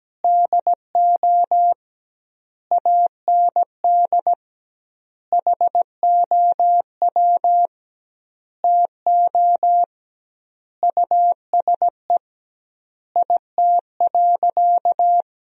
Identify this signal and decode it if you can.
Morse code, heard as a keyed tone, about 17 wpm, 700 Hz. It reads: DO AND HOW TO USE IT.
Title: 2x Word-Spacing - Sentences from Top 500 Words - 17wpm